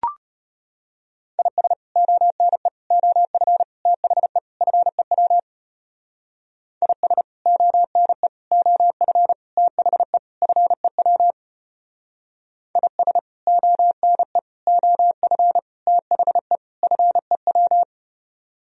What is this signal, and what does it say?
Morse code, heard as a keyed tone, about 34 wpm, 700 Hz. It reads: IS ONE OF THE FEW IS ONE OF THE FEW IS ONE OF THE FEW